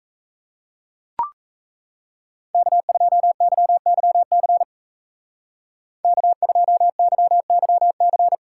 Morse code, heard as a keyed tone, about 33 wpm, 700 Hz. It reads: K2YYC K2YYC
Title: Speed Racing - USA 1x3 Call Signs 28wpm